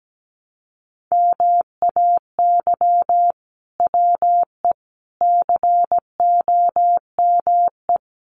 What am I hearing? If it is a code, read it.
Morse code, heard as a keyed tone, about 17 wpm, 700 Hz. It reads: MAY WE COME
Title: Sentences from Top 100 Words 17wpm